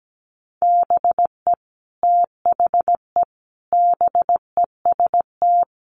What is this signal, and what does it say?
Morse code, heard as a keyed tone, about 17 wpm, 700 Hz. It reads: BE THE BEST
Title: Sentences from Top 1400 Words 17wpm